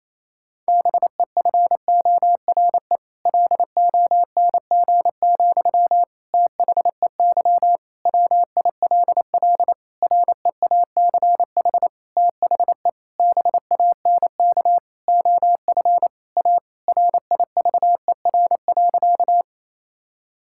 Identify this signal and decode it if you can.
Morse code, heard as a keyed tone, about 28 wpm, 700 Hz. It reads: BEFORE LONG, THEY WILL REACH THE BANK OF A RIVER.
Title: Sentences from Top 1000 Words - Review Part 6 - 28wpm